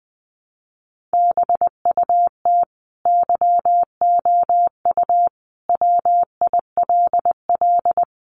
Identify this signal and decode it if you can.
Morse code, heard as a keyed tone, about 20 wpm, 700 Hz. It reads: BUT YOU WILL
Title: Sentences from Top 1200 Words 20wpm